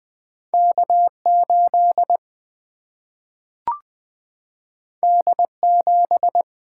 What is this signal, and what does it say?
Morse code, heard as a keyed tone, about 20 wpm, 700 Hz. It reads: K 8 D 7
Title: Single-Letter Single-Number 20wpm